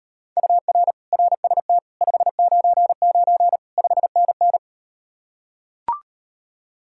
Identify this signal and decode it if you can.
Morse code, heard as a keyed tone, about 38 wpm, 700 Hz. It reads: UR RST 599 5NN